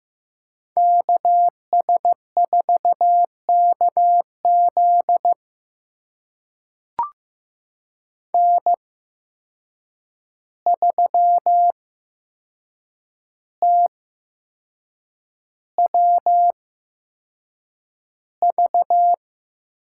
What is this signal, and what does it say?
Morse code, heard as a keyed tone, about 15 wpm, spaced as for 6 wpm, 700 Hz. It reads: KS4KZ N3TWV
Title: USA Call Signs - 5 Characters - 8x Character Spacing - 15wpm